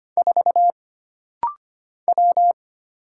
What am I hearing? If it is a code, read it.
Morse code, heard as a keyed tone, about 25 wpm, 700 Hz. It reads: 4 W
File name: Single Character (Letter:Number:Punctuation:Prosign) - Rapid-Fire - 25wpm